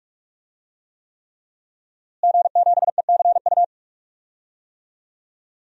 Morse code, heard as a keyed tone, about 45 wpm, 700 Hz. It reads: G7ECU
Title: DXCC-100 Call Signs - 5 Characters - Repeated 3x - 45wpm